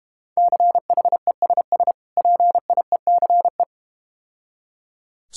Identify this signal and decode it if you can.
Morse code, heard as a keyed tone, about 32 wpm, 700 Hz. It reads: CHESS PIECE